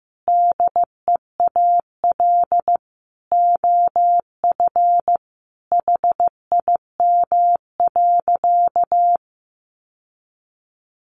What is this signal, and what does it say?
Morse code, heard as a keyed tone, about 15 wpm, 700 Hz. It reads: DEAL OF HIM.
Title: Sentences from Top 800 Words 15wpm